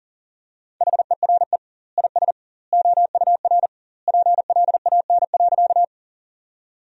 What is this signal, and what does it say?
Morse code, heard as a keyed tone, about 40 wpm, 700 Hz. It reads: HERE IS OUR PLAN.
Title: 2x Word-Spacing - Sentences from Top 500 Words - 40wpm